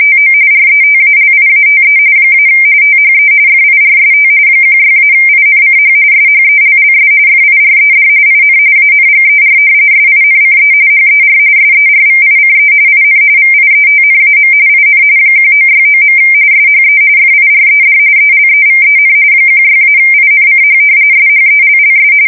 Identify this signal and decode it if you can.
RTTY, about 45.45 baud. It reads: TO RETALIATE - THE FIRST 16-INCH SHELLS FIRED IN ANGER SINCE HER SERVICE IN VIETNAM IN 1969. SHE FIRED ELEVEN 16-INCH SHELLS
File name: fsk170 / battleshipnj